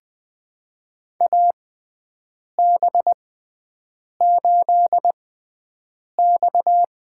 Morse code, heard as a keyed tone, about 20 wpm, spaced as for 7 wpm, 700 Hz. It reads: AB8X
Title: USA Call Signs - 6x Character Spacing - 20wpm